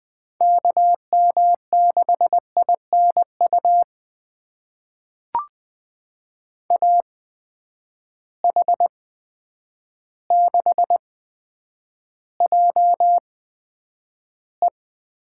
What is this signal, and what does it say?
Morse code, heard as a keyed tone, about 20 wpm, spaced as for 9 wpm, 700 Hz. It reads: KM6INU AH6JE
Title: USA Call Signs - 8x Character Spacing - 20wpm